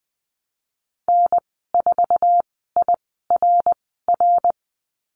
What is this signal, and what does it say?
Morse code, heard as a keyed tone, about 19 wpm, spaced as for 14 wpm, 700 Hz. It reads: N4IRR
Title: 088 Lesson 15g - Number 5 - Call Signs 20wpm